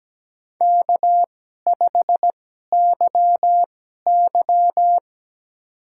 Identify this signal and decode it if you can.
Morse code, heard as a keyed tone, about 17 wpm, spaced as for 12 wpm, 700 Hz. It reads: K5YY
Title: USA Call Signs - 4 Characters - 2x Character Spacing - 17wpm